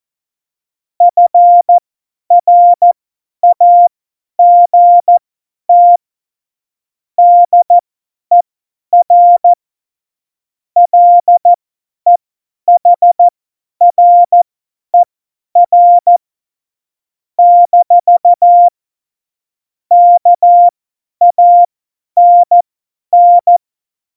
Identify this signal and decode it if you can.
Morse code, heard as a keyed tone, about 14 wpm, spaced as for 10 wpm, 700 Hz. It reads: FRAGT DER LEHRER - KANN
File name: Morsekurs / Witze / Witz 28. Mai 2019, 50 F70